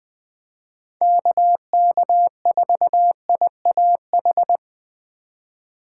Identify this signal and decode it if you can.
Morse code, heard as a keyed tone, about 20 wpm, 700 Hz. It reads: KK4IAH
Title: USA Call Signs - 3x Character Spacing - 20wpm